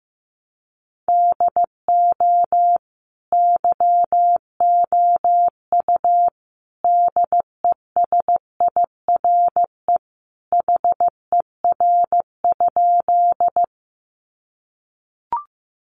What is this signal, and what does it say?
Morse code, heard as a keyed tone, about 15 wpm, 700 Hz. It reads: DO YOU DESIRE HER?
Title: Sentences from Top 1100 Words 15wpm